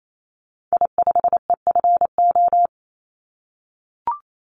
Morse code, heard as a keyed tone, about 28 wpm, 700 Hz. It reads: I5EFO
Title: Call Signs - 28wpm